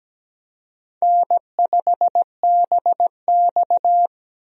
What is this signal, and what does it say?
Morse code, heard as a keyed tone, about 17 wpm, 700 Hz. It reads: N5BX
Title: USA Call Signs - 4 Characters - 17wpm